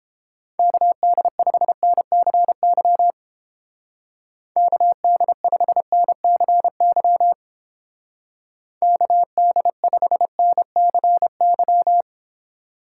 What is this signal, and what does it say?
Morse code, heard as a keyed tone, about 29 wpm, 700 Hz. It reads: KD5NCY KD5NCY KD5NCY